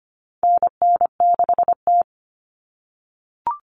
Morse code, heard as a keyed tone, about 25 wpm, 700 Hz. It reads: NN6T
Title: Call Signs - 25wpm